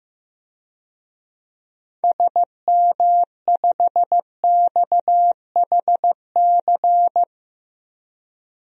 Morse code, heard as a keyed tone, about 15 wpm, 700 Hz. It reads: SM5XHC
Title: DXCC-100 Call Signs - 6 Characters - 15wpm